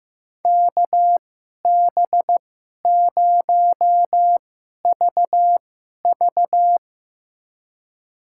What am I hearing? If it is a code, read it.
Morse code, heard as a keyed tone, about 15 wpm, spaced as for 11 wpm, 700 Hz. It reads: KB0VV
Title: USA Call Signs - 5 Characters - 2x Character Spacing - 15wpm